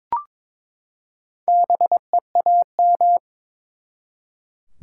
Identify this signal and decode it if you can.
Morse code, heard as a keyed tone, about 22 wpm, 700 Hz. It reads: BEAM
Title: Top 90 QSO Elements 22wpm